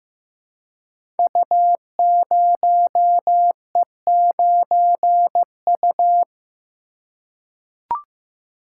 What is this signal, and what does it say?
Morse code, heard as a keyed tone, about 15 wpm, 700 Hz. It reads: U0E9U
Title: Code Groups - 5 Letter:Numbers - 15wpm